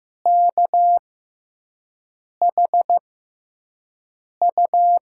Morse code, heard as a keyed tone, about 15 wpm, spaced as for 5 wpm, 700 Hz. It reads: KHU
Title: USA Call Signs - 6 Characters - 6x Character Spacing - 15wpm